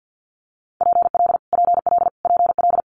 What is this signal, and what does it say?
Morse code, heard as a keyed tone, about 50 wpm, 700 Hz. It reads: PL PL PL
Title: Multi-letter Phonemes Repeated 3x 50wpm